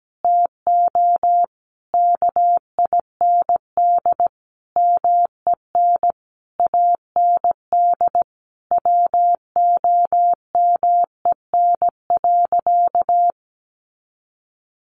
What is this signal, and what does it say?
Morse code, heard as a keyed tone, about 17 wpm, 700 Hz. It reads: TO KIND MEN AND WOMEN.
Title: Sentences from Top 1300 Words 17wpm